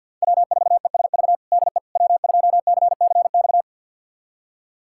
Morse code, heard as a keyed tone, about 50 wpm, 700 Hz. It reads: W4ESV DE W2CCX
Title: USA Callsign DE Callsign - 5 Characters - 50wpm